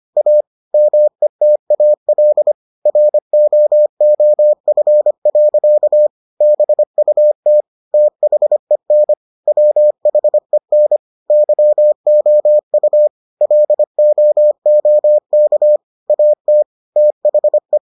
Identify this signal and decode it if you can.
Morse code, heard as a keyed tone, about 25 wpm, 600 Hz. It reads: A METAL ROOF. BUT THEN WHEN YOU LOOK AT THE